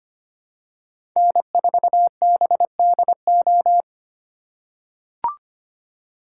Morse code, heard as a keyed tone, about 25 wpm, 700 Hz. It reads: N4BDO
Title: USA Call Signs - 5 Characters - 2x Character Spacing - 25wpm